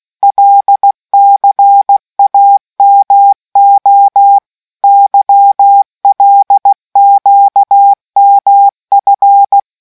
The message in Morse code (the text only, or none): LCAMO YLQMF